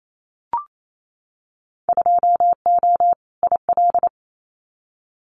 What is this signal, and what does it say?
Morse code, heard as a keyed tone, about 28 wpm, 700 Hz. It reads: 2O IL
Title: Winter Field Day Exchanges 28wpm